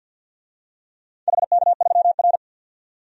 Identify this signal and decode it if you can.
Morse code, heard as a keyed tone, about 52 wpm, 700 Hz. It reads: HK3R
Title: DXCC-100 Call Signs - 4 Characters - 50wpm